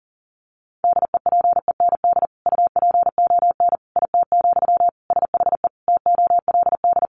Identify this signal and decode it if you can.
Morse code, heard as a keyed tone, about 40 wpm, 700 Hz. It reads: DEPEND UPON IT, SHE TOLD
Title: Sentences from Top 1300 Words 40wpm